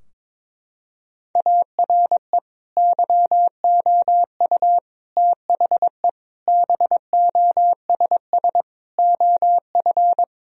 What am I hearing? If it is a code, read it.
Morse code, heard as a keyed tone, about 22 wpm, 700 Hz. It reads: ARE YOU THE BOSS OF
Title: Sentences from Top 1000 Words - Review Part 4 - 22wpm